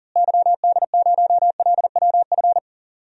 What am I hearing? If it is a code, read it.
Morse code, heard as a keyed tone, about 40 wpm, 700 Hz. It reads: YD0LWF
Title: DXCC-100 Call Signs - 6 Characters - Repeated 3x - 40wpm